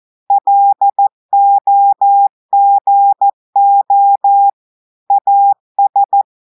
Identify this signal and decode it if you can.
Morse code, heard as a keyed tone, about 14 wpm, 800 Hz. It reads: LOGO AS
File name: audios / portugais / portugais texte en clair 2 vitesse 14 mots minutes